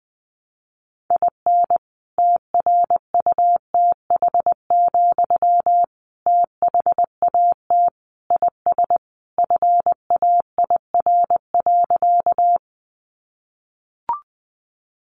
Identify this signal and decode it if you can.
Morse code, heard as a keyed tone, about 20 wpm, 700 Hz. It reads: IN TRUTH, THAT IS FAIR.
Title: Sentences from Top 1300 Words 20wpm